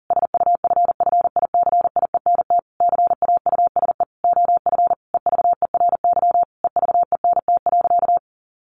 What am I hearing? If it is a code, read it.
Morse code, heard as a keyed tone, about 40 wpm, 700 Hz. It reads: SUFFICIENT CAUSE OF EVERY EVENT.
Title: Sentences from Top 1400 Words 40wpm